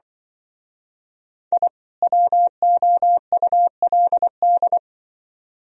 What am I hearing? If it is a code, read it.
Morse code, heard as a keyed tone, about 24 wpm, 700 Hz. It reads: I WOULD
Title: Speed-Racing - Sets of 2 Words 20wpm